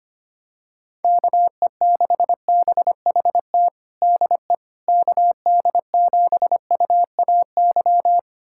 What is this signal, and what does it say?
Morse code, heard as a keyed tone, about 25 wpm, 700 Hz. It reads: KE6BHT DE KD7UAY
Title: USA Callsign DE Callsign 25wpm